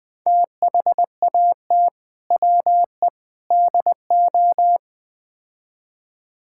Morse code, heard as a keyed tone, about 20 wpm, 700 Hz. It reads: THAT WE DO
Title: Build A Phrase - 4-Word Phrases - 20wpm